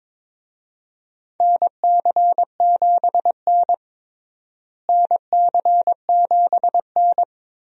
Morse code, heard as a keyed tone, about 22 wpm, 700 Hz. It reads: NC7N NC7N
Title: USA Call Signs - 4 Characters - Repeated 3x - 22wpm